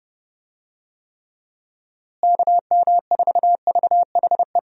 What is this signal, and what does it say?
Morse code, heard as a keyed tone, about 30 wpm, 700 Hz. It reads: KM4VHE
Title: USA Call Signs - 6 Characters - Repeated 2x - 30wpm